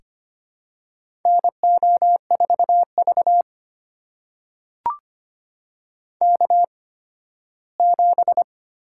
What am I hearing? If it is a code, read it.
Morse code, heard as a keyed tone, about 25 wpm, spaced as for 15 wpm, 700 Hz. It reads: NO4V K7
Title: USA Call Signs - 4 Characters - 8x Character Spacing - 25wpm